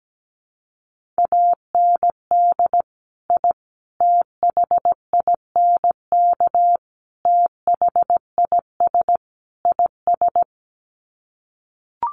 Morse code, heard as a keyed tone, about 17 wpm, 700 Hz. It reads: AND I THINK THIS IS
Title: Sets of 5 Words Encore 17wpm